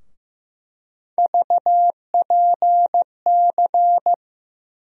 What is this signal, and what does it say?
Morse code, heard as a keyed tone, about 15 wpm, 700 Hz. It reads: VPC
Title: USA Call Sign Postfixes - 3 Letters 15wpm